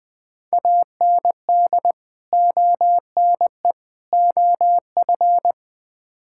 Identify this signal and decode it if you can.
Morse code, heard as a keyed tone, about 20 wpm, 700 Hz. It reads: AND ONE OF
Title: Build A Phrase - 5-Word Phrases - 20wpm